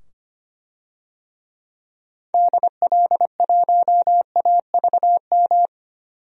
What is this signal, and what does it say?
Morse code, heard as a keyed tone, about 25 wpm, 700 Hz. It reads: DL1AVM
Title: DXCC-100 Call Signs - 6 Characters - 25wpm